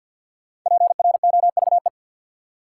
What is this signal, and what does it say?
Morse code, heard as a keyed tone, about 47 wpm, 700 Hz. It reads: PROVE
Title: Sentences from Top 1600 Words - New Word Drill Alt - 50wpm